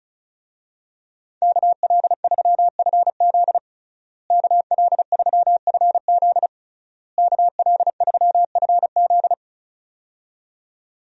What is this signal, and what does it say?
Morse code, heard as a keyed tone, about 36 wpm, 700 Hz. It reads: KL3FZ KL3FZ KL3FZ
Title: USA Call Signs - 5 Characters - Repeated 3x - 35wpm